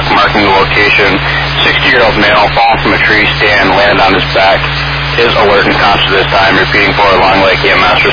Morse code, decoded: none

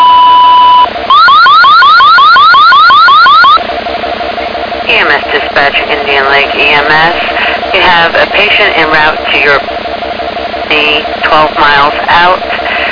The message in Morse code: none

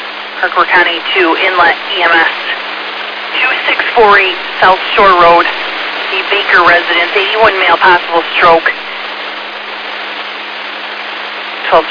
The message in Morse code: none

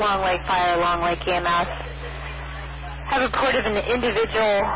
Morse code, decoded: none

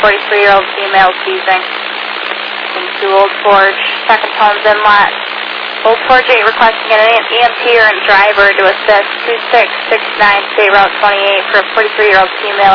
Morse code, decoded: none

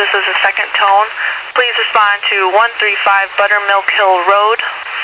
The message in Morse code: none

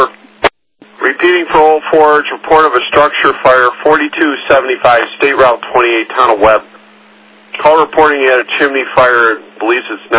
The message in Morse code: none

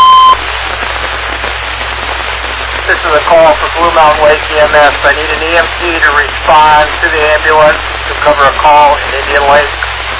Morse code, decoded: none

